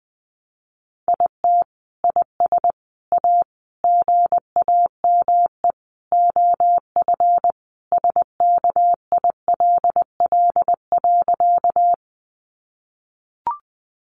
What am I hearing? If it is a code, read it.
Morse code, heard as a keyed tone, about 20 wpm, 700 Hz. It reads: IT IS A GAME OF SKILL.
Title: Sentences from Top 1200 Words 20wpm